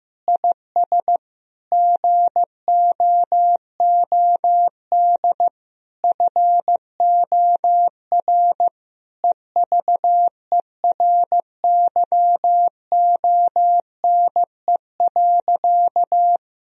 Morse code, heard as a keyed tone, about 16 wpm, 700 Hz. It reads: IS GOOD FOR EVERYONE.
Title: Sentences from Top 1000 Words - Review Part 4 - 15wpm